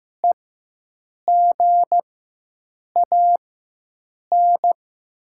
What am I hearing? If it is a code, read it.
Morse code, heard as a keyed tone, about 15 wpm, spaced as for 7 wpm, 700 Hz. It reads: EGAN
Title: 4x Character Spacing - Top 300 Words - 15wpm